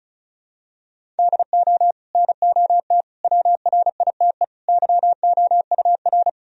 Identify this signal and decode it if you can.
Morse code, heard as a keyed tone, about 34 wpm, 700 Hz. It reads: DO NOT WRITE YOUR
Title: Sentences from Top 1000 Words - Review Part 1 - 35wpm